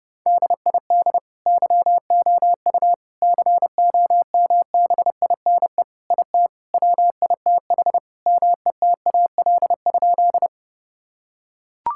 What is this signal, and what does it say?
Morse code, heard as a keyed tone, about 30 wpm, 700 Hz. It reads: DID YOU COMBINE IT WITH METAL?
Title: Sentences from Top 1600 Words 30wpm